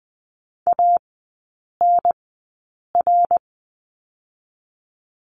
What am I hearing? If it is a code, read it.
Morse code, heard as a keyed tone, about 20 wpm, spaced as for 8 wpm, 700 Hz. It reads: ANR
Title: 093 Lesson 16e - Letter c - Three Characters 20wpm